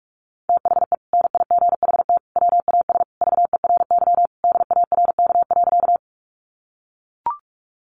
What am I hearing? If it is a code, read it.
Morse code, heard as a keyed tone, about 44 wpm, 700 Hz. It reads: THE NIGHT WAS VERY DARK.
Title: Sentences from Top 500 Words 45wpm